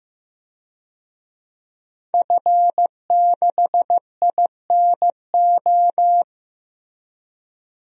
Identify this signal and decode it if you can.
Morse code, heard as a keyed tone, about 15 wpm, 700 Hz. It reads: F6INO
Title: DXCC-100 Call Signs - 5 Characters - Repeated 2x - 15wpm